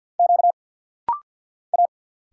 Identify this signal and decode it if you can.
Morse code, heard as a keyed tone, about 47 wpm, 700 Hz. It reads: Q A
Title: Single Letters - Rapid-Fire 50wpm